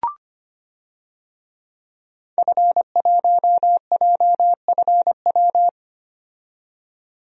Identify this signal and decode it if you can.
Morse code, heard as a keyed tone, about 25 wpm, 700 Hz. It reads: F1JFW
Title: DXCC-100 Call Signs - 5 Characters - 25wpm